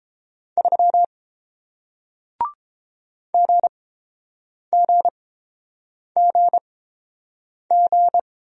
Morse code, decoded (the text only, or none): 3 GGGG